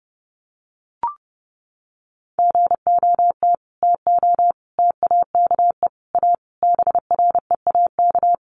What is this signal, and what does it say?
Morse code, heard as a keyed tone, about 30 wpm, 700 Hz. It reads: GOT TO TAKE A BREAK
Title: Sets of 5 Words Encore 30wpm